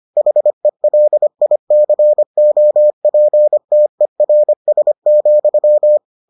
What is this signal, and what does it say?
Morse code, heard as a keyed tone, about 25 wpm, 600 Hz. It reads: HELICOPTERS,